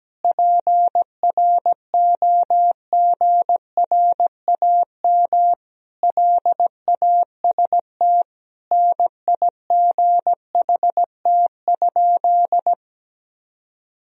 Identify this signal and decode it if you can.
Morse code, heard as a keyed tone, about 17 wpm, 700 Hz. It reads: PROGRAM LAST NIGHT?
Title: Sentences from Top 1000 Words - Review Part 4 - 17wpm